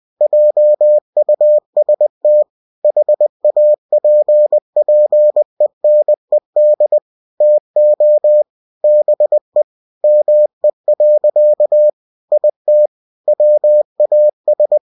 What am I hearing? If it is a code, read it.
Morse code, heard as a keyed tone, about 20 wpm, 600 Hz. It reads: JUST HAPPENED TO BE ME. IT WAS
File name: cw-books / winnie-the-pooh / winnie-20wpm-page-0107